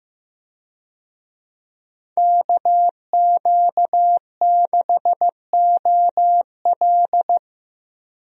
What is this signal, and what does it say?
Morse code, heard as a keyed tone, about 15 wpm, 700 Hz. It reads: KQ6OL
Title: USA Call Signs - 5 Characters - Repeated 3x - 15wpm